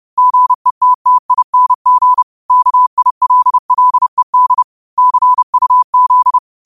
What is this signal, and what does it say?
Morse code, heard as a keyed tone, about 30 wpm, 1000 Hz. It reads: GETTING KILLED CUZ